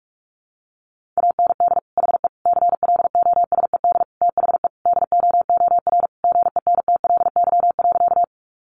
Morse code, heard as a keyed tone, about 44 wpm, 700 Hz. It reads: AND HE CLOSED THE DOOR GENTLY.
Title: Sentences from Top 1500 Words 45wpm